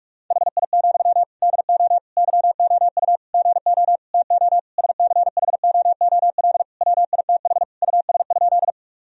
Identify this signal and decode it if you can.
Morse code, heard as a keyed tone, about 41 wpm, 700 Hz. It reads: HI, DO YOU GO TO SCHOOL WITH US?
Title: Sentences from Top 1000 Words - Review Part 3 - 45wpm